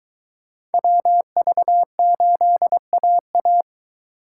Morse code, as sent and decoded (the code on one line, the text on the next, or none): .-- ...- ---.. .- .-
WV8AA